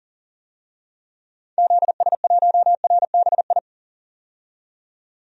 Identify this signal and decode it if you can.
Morse code, heard as a keyed tone, about 40 wpm, 700 Hz. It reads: ZS1RBI
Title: DXCC-100 Call Signs - 6 Characters - 40wpm